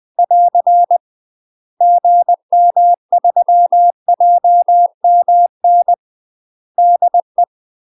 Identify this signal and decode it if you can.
Morse code, heard as a keyed tone, about 20 wpm, 700 Hz. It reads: <AR> GM3JMN DE
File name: UK MASTER QSOs TRAINING / 20WPM / UK MASTER QSOs 0050